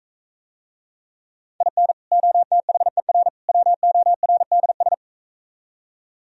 Morse code, heard as a keyed tone, about 44 wpm, 700 Hz. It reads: IN OTHER WORDS